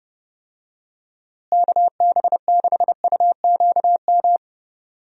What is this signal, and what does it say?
Morse code, heard as a keyed tone, about 30 wpm, 700 Hz. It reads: KB6UQM
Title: USA Call Signs - 6 Characters - Repeated 2x - 30wpm